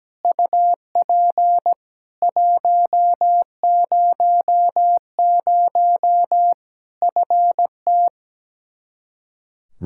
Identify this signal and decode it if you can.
Morse code, heard as a keyed tone, about 17 wpm, 700 Hz. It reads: UP 100 FT
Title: Practice QSO - Exchange 2 - Info - 17wpm